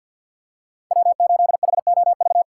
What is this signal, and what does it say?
Morse code, heard as a keyed tone, about 46 wpm, 700 Hz. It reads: W8HOV
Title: USA Call Signs - 5 Characters - 2x Character Spacing - 50wpm